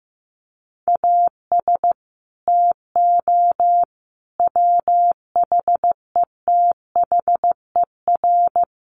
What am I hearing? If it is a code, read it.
Morse code, heard as a keyed tone, about 15 wpm, 700 Hz. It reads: AS TO WHETHER
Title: Sets of 5 Words Encore 15wpm